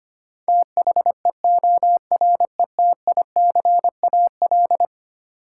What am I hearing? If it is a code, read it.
Morse code, heard as a keyed tone, about 25 wpm, 700 Hz. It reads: THEORETICAL
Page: morse-code-ninja-practice-25wpm